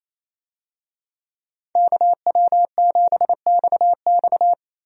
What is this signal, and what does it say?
Morse code, heard as a keyed tone, about 27 wpm, 700 Hz. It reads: KW7XX
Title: USA Call Signs - 5 Characters - 28wpm